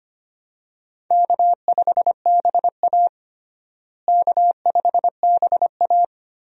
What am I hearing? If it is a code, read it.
Morse code, heard as a keyed tone, about 25 wpm, 700 Hz. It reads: K5BA K5BA
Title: USA Call Signs - 4 Characters - Repeated 2x - 25wpm